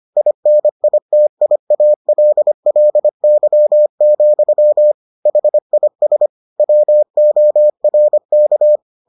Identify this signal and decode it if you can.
Morse code, heard as a keyed tone, about 25 wpm, 600 Hz. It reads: INITIALLY, HIS WORK